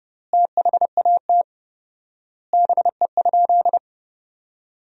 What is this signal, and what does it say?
Morse code, heard as a keyed tone, about 30 wpm, 700 Hz. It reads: THAT BE?
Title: 4x Word-Spacing - Sentences from Top 100 Words - 30wpm